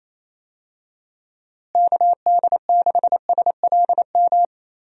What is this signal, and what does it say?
Morse code, heard as a keyed tone, about 28 wpm, 700 Hz. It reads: KD6SLM
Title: USA Call Signs - 6 Characters - Repeated 2x - 28wpm